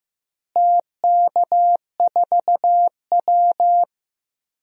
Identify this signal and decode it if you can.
Morse code, heard as a keyed tone, about 15 wpm, 700 Hz. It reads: TK4W
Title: DXCC-100 Call Signs - 4 Characters - Repeated 3x - 15wpm